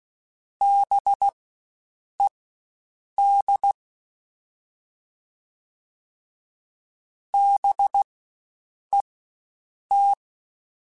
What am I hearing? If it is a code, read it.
Morse code, heard as a keyed tone, about 16 wpm, spaced as for 7 wpm, 800 Hz. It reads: BED BET
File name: Spaced Alphas / Spaced Alphas mp3s / Bs sp @15